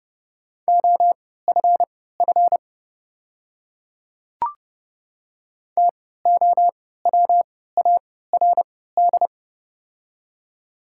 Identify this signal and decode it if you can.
Morse code, heard as a keyed tone, about 30 wpm, spaced as for 17 wpm, 700 Hz. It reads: OFF TOWARD
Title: 3x Character Spacing - Top 500 Words - 30wpm